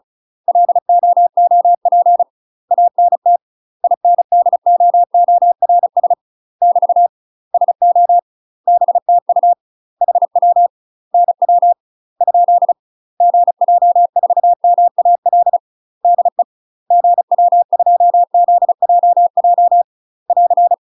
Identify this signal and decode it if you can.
Morse code, heard as a keyed tone, about 34 wpm, 700 Hz. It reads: LOOP ANT INDOORS <BT> SO BTU HW NW ? GJ4MAL DE GW2ZJJ <AR>
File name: UK MASTER QSOs TRAINING / 35WPM / UK MASTER QSOs 0019